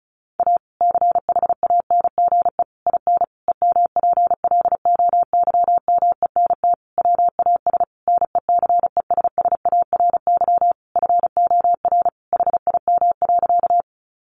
Morse code, heard as a keyed tone, about 34 wpm, 700 Hz. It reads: A CHANGE IN EMPLOYMENT WAS NECESSARY FOR HIM.
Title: Sentences from Top 1500 Words 35wpm